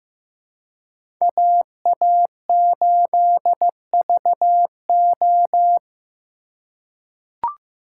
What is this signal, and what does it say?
Morse code, heard as a keyed tone, about 16 wpm, 700 Hz. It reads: AA8VO
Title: USA Call Signs - 5 Characters - 4x Character Spacing - 15wpm